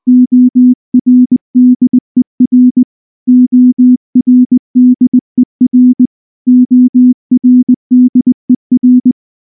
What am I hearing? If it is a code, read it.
Morse code, heard as a keyed tone, about 20 wpm, 250 Hz. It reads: ORDER ORDER ORDER
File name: morse / morse-training-20WPM-7